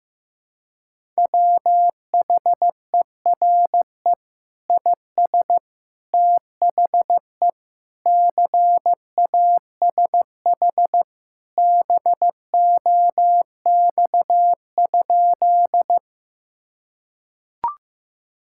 Morse code, heard as a keyed tone, about 15 wpm, 700 Hz. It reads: WHERE IS THE CASH BOX?
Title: Sentences from Top 1000 Words - Review Part 5 - 15wpm